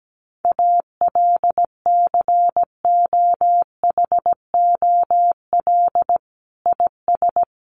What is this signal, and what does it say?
Morse code, heard as a keyed tone, about 17 wpm, 700 Hz. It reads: ALCOHOL IS
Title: Sentences from Top 1400 Words 17wpm